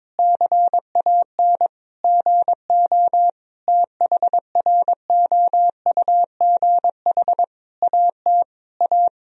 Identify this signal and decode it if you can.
Morse code, heard as a keyed tone, about 22 wpm, 700 Hz. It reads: CAN GO THROUGH AT A
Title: Sentences from Top 1000 Words - Review Part 1 - 22wpm